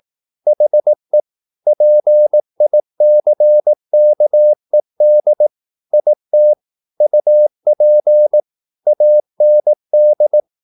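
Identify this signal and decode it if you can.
Morse code, heard as a keyed tone, about 18 wpm, 600 Hz. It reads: HE PICKED IT UP AND
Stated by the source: Morse practice at 18 wpm, English text, from CW med Gnister.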